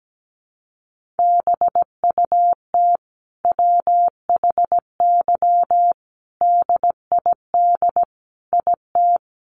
Morse code, heard as a keyed tone, about 17 wpm, 700 Hz. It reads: BUT WHY DID IT
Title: Sentences from Top 1400 Words 17wpm